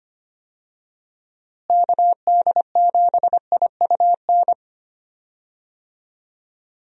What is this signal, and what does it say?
Morse code, heard as a keyed tone, about 25 wpm, 700 Hz. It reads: KD7IUN
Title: USA Call Signs - 6 Characters - Repeated 3x - 25wpm